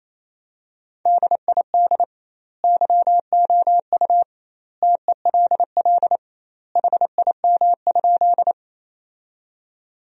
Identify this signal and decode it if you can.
Morse code, heard as a keyed tone, about 28 wpm, 700 Hz. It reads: DID YOU TELL HIM?
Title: 2x Word-Spacing - Sentences from Top 200 Words - 28wpm